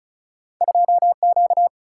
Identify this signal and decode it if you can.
Morse code, heard as a keyed tone, about 34 wpm, 700 Hz. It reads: 2 Q
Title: Single-Letter Single-Number 35wpm